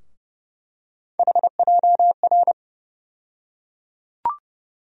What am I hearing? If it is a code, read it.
Morse code, heard as a keyed tone, about 30 wpm, 700 Hz. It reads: HJR
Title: USA Call Sign Postfixes - 3 Letters 30wpm